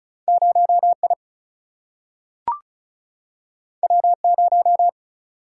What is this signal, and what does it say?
Morse code, heard as a keyed tone, about 34 wpm, 700 Hz. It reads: 0I W0
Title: Single-Letter Single-Number 35wpm